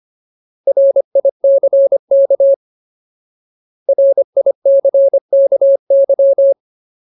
Morse code, heard as a keyed tone, about 25 wpm, 550 Hz. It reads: RICK RICKY
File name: Ultimate Word List / 25WPM Ultimate